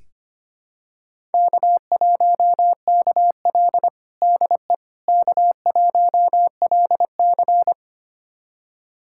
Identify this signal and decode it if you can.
Morse code, heard as a keyed tone, about 25 wpm, 700 Hz. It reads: K1KL DE K1LC
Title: USA Callsign DE Callsign - 4 Characters - 25wpm